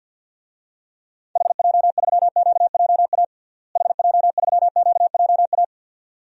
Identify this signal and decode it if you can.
Morse code, heard as a keyed tone, about 52 wpm, 700 Hz. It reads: HJ3QJA HJ3QJA
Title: DXCC-100 Call Signs - 6 Characters - Repeated 2x - 50wpm